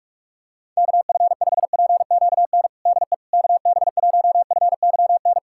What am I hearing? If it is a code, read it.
Morse code, heard as a keyed tone, about 46 wpm, 700 Hz. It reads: KF5PQN DE KB1FYN
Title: USA Callsign DE Callsign - 6 Characters - 45wpm